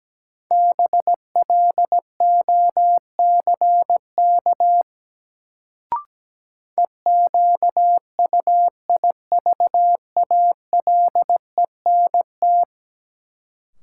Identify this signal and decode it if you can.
Morse code, heard as a keyed tone, about 17 wpm, 700 Hz. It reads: BLOCK EQUIVALENT
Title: Sentences from Top 1600 Words - New Word Drill - 17wpm